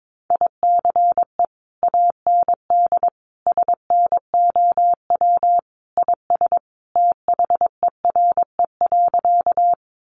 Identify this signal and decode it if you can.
Morse code, heard as a keyed tone, about 22 wpm, 700 Hz. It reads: ICE AND SNOW IS THERE.
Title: Sentences from Top 1000 Words 22wpm